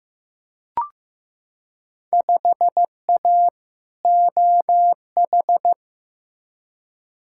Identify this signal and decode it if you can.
Morse code, heard as a keyed tone, about 15 wpm, 700 Hz. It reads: 5A OH
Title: Field Day Exchanges Encore - 15wpm